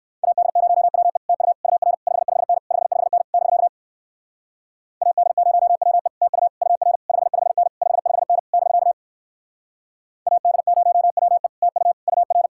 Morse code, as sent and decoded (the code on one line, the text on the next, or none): .- -.. ----- .-- . / - ..- / ..- .-. / ..... ..... -. / ..... ..... -. / -...-.- / .- -.. ----- .-- . / - ..- / ..- .-. / ..... ..... -. / ..... ..... -. / -...-.- / .- -.. ----- .-- . / - ..- / ..- .-.
AD0WE TU UR 55N 55N <BK> AD0WE TU UR 55N 55N <BK> AD0WE TU UR